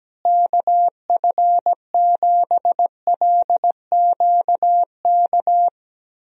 Morse code, heard as a keyed tone, about 17 wpm, 700 Hz. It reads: KF7LQK
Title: USA Call Signs - 6 Characters - Repeated 3x - 17wpm